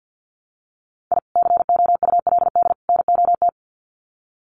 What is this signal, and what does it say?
Morse code, heard as a keyed tone, about 48 wpm, 700 Hz. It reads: I COULD NOT
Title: Sets of 3 Words 50wpm